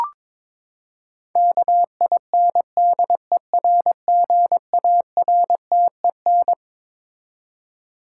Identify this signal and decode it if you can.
Morse code, heard as a keyed tone, about 22 wpm, 700 Hz. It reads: KINDERGARTEN
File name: Twelve Letter Words - 22wpm